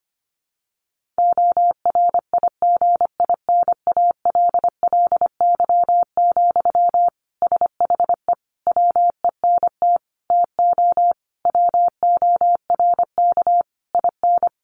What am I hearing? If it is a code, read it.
Morse code, heard as a keyed tone, about 25 wpm, 700 Hz. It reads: ORIGINALLY, SHE WENT TO WORK IN
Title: Sentences from Top 1500 Words 25wpm